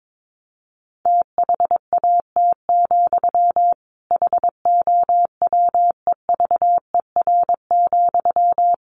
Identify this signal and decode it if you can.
Morse code, heard as a keyed tone, about 22 wpm, 700 Hz. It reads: THAT, HOWEVER,
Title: Sentences from Top 1300 Words 22wpm